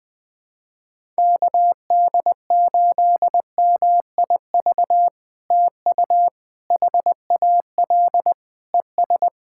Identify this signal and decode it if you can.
Morse code, heard as a keyed tone, about 20 wpm, 700 Hz. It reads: KD8MIV TU HAL ES
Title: POTA QSO Practice - 20wpm